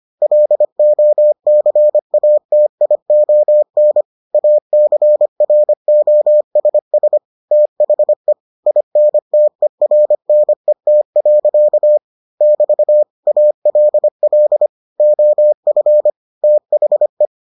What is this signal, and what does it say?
Morse code, heard as a keyed tone, about 25 wpm, 600 Hz. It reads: LOCATION ACROSS THE INTERNET. <BT> ALL OF THE